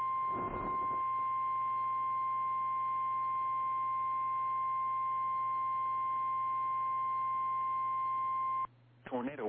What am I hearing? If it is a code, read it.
Emergency Alert System (SAME) tones (the 1050 Hz attention tone), with speech. It is heard from 0.6 to 8.7 s.
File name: eas / KKPRKXPN / KKPRKXPN